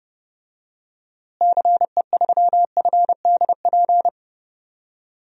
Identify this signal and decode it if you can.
Morse code, heard as a keyed tone, about 30 wpm, 700 Hz. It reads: CE3FDP
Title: DXCC-100 Call Signs - 6 Characters - Repeated 2x - 30wpm